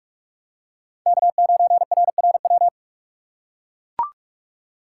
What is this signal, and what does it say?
Morse code, heard as a keyed tone, about 44 wpm, 700 Hz. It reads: K9RRW
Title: USA Call Signs - 8x Character Spacing - 45wpm